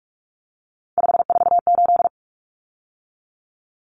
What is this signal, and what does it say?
Morse code, heard as a keyed tone, about 44 wpm, 700 Hz. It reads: 548
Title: Numbers - 3-Digit 45wpm